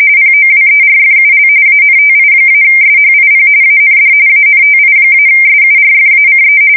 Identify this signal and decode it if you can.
RTTY, about 45.45 baud. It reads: POWERFUL ANTI-SHIP AND ANTI-SURFACE
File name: fsk170 / battleshipnj